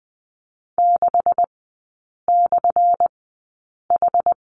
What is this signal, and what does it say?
Morse code, heard as a keyed tone, about 20 wpm, spaced as for 8 wpm, 700 Hz. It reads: 6/H
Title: 178 Lesson 29e - Slash - Three Characters 20wpm